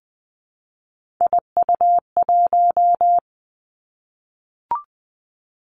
Morse code, heard as a keyed tone, about 20 wpm, 700 Hz. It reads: IU1
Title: Three Challenging Letter-Number Combinations 20wpm